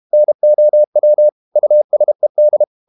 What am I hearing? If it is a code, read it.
Morse code, heard as a keyed tone, about 32 wpm, 600 Hz. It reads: NOW USED
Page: Never 31wpm | CW med Gnister